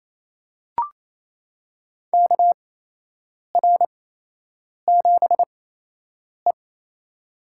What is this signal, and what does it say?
Morse code, heard as a keyed tone, about 28 wpm, spaced as for 8 wpm, 700 Hz. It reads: KR7E
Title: USA Call Signs - 4 Characters - 8x Character Spacing - 28wpm